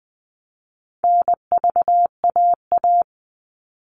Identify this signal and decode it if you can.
Morse code, heard as a keyed tone, about 20 wpm, 700 Hz. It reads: NVAA
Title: All Cut Numbers - 4-Digit 20wpm